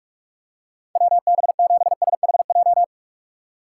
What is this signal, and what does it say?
Morse code, heard as a keyed tone, about 45 wpm, 690 Hz. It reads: WB7SHJ
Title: USA Call Signs - 6 Characters - 3x Character Spacing - 45wpm